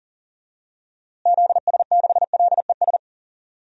Morse code, heard as a keyed tone, about 40 wpm, 700 Hz. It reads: ZS6LES
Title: DXCC-100 Call Signs - 6 Characters - Repeated 3x - 40wpm